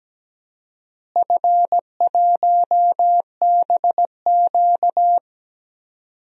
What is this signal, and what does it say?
Morse code, heard as a keyed tone, about 17 wpm, 700 Hz. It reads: F1BQ
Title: DXCC-100 Call Signs - 4 Characters - Repeated 2x - 17wpm